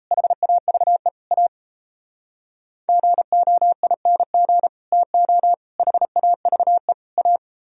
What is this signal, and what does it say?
Morse code, heard as a keyed tone, about 36 wpm, 700 Hz. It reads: HAVE A GOING TO HAVE A